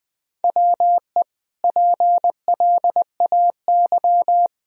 Morse code, heard as a keyed tone, about 20 wpm, 700 Hz. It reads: WE PLAY
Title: Sentences from Top 1000 Words - Review Part 1 - 20wpm